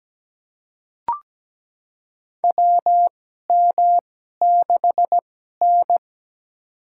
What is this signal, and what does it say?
Morse code, heard as a keyed tone, about 17 wpm, spaced as for 12 wpm, 700 Hz. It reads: WM6N
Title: USA Call Signs - 4 Characters - 2x Character Spacing - 17wpm